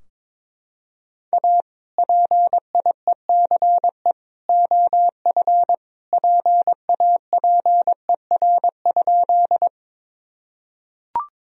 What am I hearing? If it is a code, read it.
Morse code, heard as a keyed tone, about 21 wpm, 700 Hz. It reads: A PIECE OF PAPER?
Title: Sentences from Top 1000 Words - Review Part 2 - 22wpm